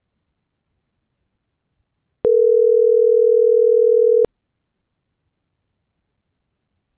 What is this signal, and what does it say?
Ringback tone (North America), 440 + 480 Hz.